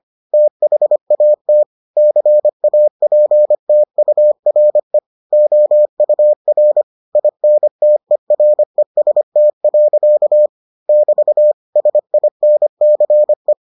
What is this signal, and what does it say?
Morse code, heard as a keyed tone, about 25 wpm, 600 Hz. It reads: THAT CAPTURE OUR INTEREST. <BT> SINCE